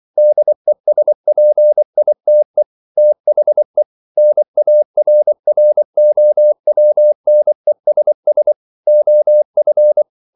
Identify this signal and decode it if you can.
Morse code, heard as a keyed tone, about 24 wpm, 600 Hz. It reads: DESPITE THE NARROWNESS OF